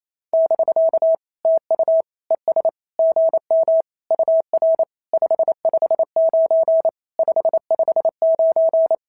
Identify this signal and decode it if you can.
Morse code, heard as a keyed tone, about 28 wpm, 650 Hz. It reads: <BK> TU ES GM UR 559 559
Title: SOTA QSO Practice 28wpm